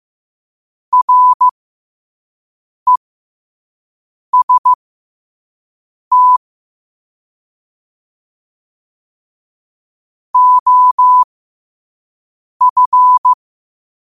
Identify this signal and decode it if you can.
Morse code, heard as a keyed tone, about 15 wpm, spaced as for 5 wpm, 1000 Hz. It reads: REST OF
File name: qotdfiles / 5wpm